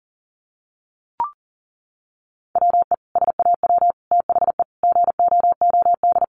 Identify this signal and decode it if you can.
Morse code, heard as a keyed tone, about 40 wpm, 700 Hz. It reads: WE SAW THE GOOD
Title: Sentences from Top 1100 Words 40wpm